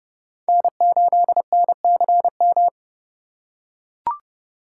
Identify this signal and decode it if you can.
Morse code, heard as a keyed tone, about 30 wpm, 700 Hz. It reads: N8NCM
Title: USA Call Signs - 5 Characters - 2x Character Spacing - 30wpm